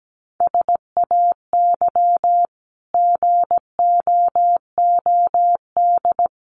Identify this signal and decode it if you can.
Morse code, heard as a keyed tone, about 17 wpm, 700 Hz. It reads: SAY GOOD
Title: Sentences from Top 300 Words 17wpm